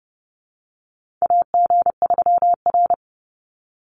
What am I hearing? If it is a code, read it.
Morse code, heard as a keyed tone, about 30 wpm, 700 Hz. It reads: AG3R
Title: Call Signs Repeated 3x - 30wpm